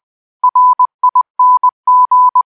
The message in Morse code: RING